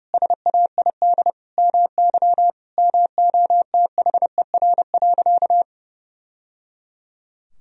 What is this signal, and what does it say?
Morse code, heard as a keyed tone, about 32 wpm, 700 Hz. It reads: SAID MY MOTHER.
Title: Sentences from Top 1000 Words - Review Part 4 - 30wpm